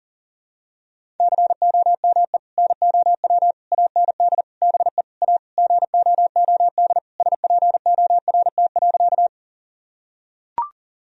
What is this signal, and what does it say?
Morse code, heard as a keyed tone, about 40 wpm, 700 Hz. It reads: COME NOW AND BE A GOOD SPORT.
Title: Sentences from Top 1000 Words - Review Part 4 - 40wpm